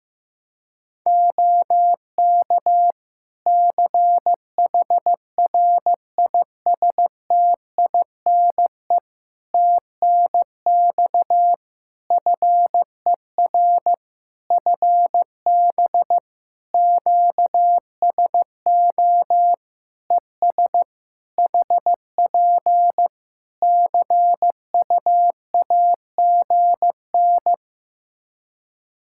Morse code, heard as a keyed tone, about 15 wpm, 700 Hz. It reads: OK CHRISTINE TNX FER FB QSO ES HP CUAGN
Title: Practice QSO - Exchange 4 - Ending - 15wpm